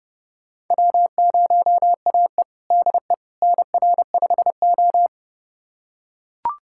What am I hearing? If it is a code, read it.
Morse code, heard as a keyed tone, about 30 wpm, 700 Hz. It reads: W0AE DE NR5O
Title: USA Callsign DE Callsign - 4 Characters - 30wpm